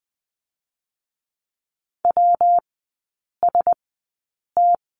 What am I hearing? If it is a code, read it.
Morse code, heard as a keyed tone, about 20 wpm, 700 Hz. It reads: W S T
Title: 114 Lesson 19e - Letter w - Three Characters 20wpm